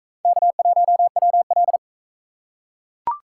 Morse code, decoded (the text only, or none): K1WL